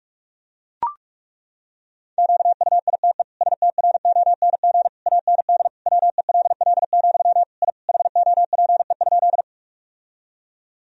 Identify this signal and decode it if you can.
Morse code, heard as a keyed tone, about 41 wpm, 700 Hz. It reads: QUITE STRONG AND WELL, I HOPE?
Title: Sentences from Top 1000 Words - Review Part 3 - 45wpm